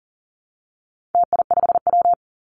Morse code, heard as a keyed tone, about 40 wpm, 700 Hz. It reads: TI5W
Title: Call Signs Repeated 2x - 40wpm